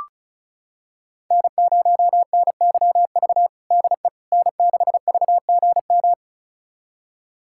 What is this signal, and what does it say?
Morse code, heard as a keyed tone, about 36 wpm, 700 Hz. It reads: N0NYV DE N6VGM
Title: USA Callsign DE Callsign - 5 Characters - 35wpm